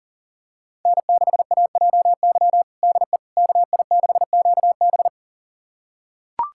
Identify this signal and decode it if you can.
Morse code, heard as a keyed tone, about 37 wpm, 700 Hz. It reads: N6AJY DE KI6QB
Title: USA Callsign DE Callsign - 5 Characters - 40wpm